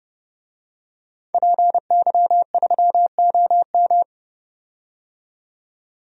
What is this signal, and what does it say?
Morse code, heard as a keyed tone, about 30 wpm, 700 Hz. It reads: PY3OM
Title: DXCC-100 Call Signs - 5 Characters - 30wpm